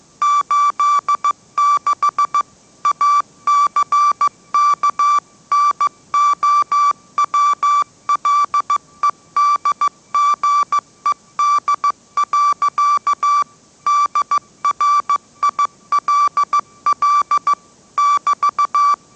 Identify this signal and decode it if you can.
Morse code, heard as a keyed tone, about 15 wpm, 1160 Hz. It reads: 86 ACKNOWLEDGED. DRILL <BT>
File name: mrx / mrx065 16wpm